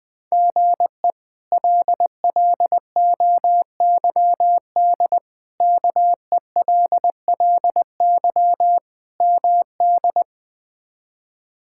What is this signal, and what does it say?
Morse code, heard as a keyed tone, about 20 wpm, 700 Hz. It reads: GE LLOYD KELLY MD
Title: K1USN SST QSO Practice - 20wpm